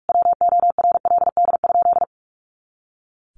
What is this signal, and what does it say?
Morse code, heard as a keyed tone, about 44 wpm, 700 Hz. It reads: WORLD?
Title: Sentences from Top 200 Words 45wpm